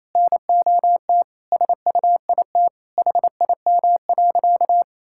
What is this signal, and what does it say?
Morse code, heard as a keyed tone, about 27 wpm, 700 Hz. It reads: NOT SUIT HIM.
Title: Sentences from Top 1600 Words 28wpm